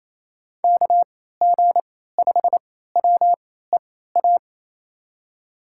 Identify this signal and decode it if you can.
Morse code, heard as a keyed tone, about 28 wpm, spaced as for 16 wpm, 700 Hz. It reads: KG5WEA
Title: USA Call Signs - 6 Characters - 3x Character Spacing - 28wpm